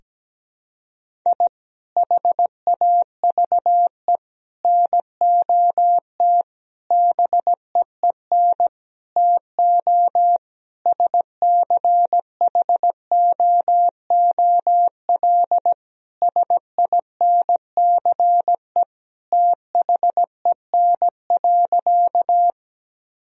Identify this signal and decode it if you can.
Morse code, heard as a keyed tone, about 17 wpm, 700 Hz. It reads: I HAVE NOT BEEN TO SCHOOL SINCE THEN.
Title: Sentences from Top 1000 Words - Review Part 1 - 17wpm